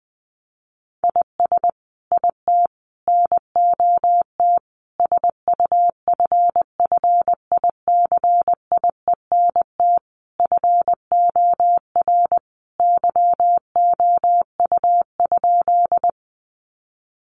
Morse code, heard as a keyed tone, about 20 wpm, 700 Hz. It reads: IS IT NOT SUFFICIENT FOR YOU?
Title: Sentences from Top 1400 Words 20wpm